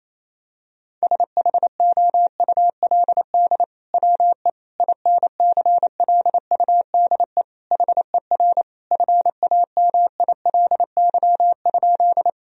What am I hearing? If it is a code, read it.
Morse code, heard as a keyed tone, about 28 wpm, 700 Hz. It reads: SHOULD WE INCLUDE HER FAMILY?
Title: Sentences from Top 1000 Words - Review Part 3 - 28wpm